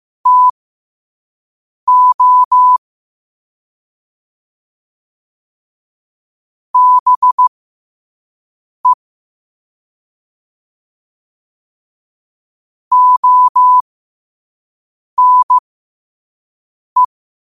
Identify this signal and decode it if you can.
Morse code, heard as a keyed tone, about 15 wpm, spaced as for 5 wpm, 1000 Hz. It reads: TO BE ONE
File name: qotdfiles / 5wpm